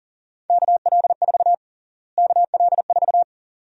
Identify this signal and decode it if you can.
Morse code, heard as a keyed tone, about 40 wpm, 700 Hz. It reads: KL4 KL4
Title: USA Call Sign Prefixes - Repeated 2x - 40wpm